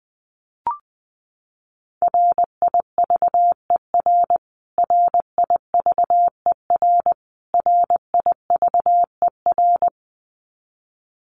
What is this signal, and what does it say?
Morse code, heard as a keyed tone, about 19 wpm, 700 Hz. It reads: RIVER RIVER RIVER
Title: Top 500 Words Repeated 20wpm